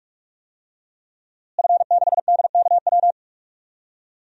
Morse code, heard as a keyed tone, about 46 wpm, 700 Hz. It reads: F6DKW
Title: DXCC-100 Call Signs - 5 Characters - Repeated 3x - 45wpm